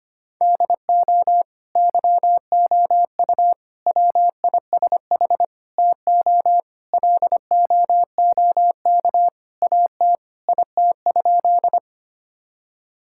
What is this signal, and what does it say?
Morse code, heard as a keyed tone, about 25 wpm, 700 Hz. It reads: DO YOU WISH TO LOOK AT IT?
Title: Sentences from Top 1600 Words 25wpm